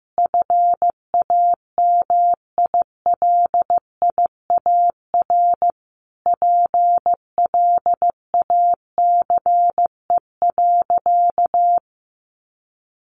Morse code, heard as a keyed tone, about 15 wpm, 700 Hz. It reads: FAMILIAR PLACE.
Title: Sentences from Top 1500 Words 15wpm